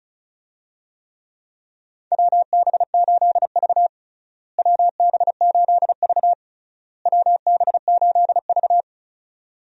Morse code, heard as a keyed tone, about 36 wpm, 700 Hz. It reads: WB8V WB8V WB8V
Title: USA Call Signs - 4 Characters - Repeated 3x - 35wpm